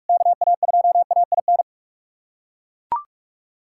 Morse code, heard as a keyed tone, about 41 wpm, 700 Hz. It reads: KA2AIN
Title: USA Callsign DE Callsign - 6 Characters - 45wpm